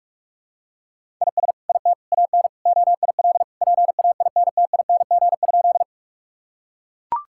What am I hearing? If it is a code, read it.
Morse code, heard as a keyed tone, about 42 wpm, 700 Hz. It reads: IS IT AN OIL PAINTING?
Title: Sentences from Top 1000 Words - Review Part 6 - 45wpm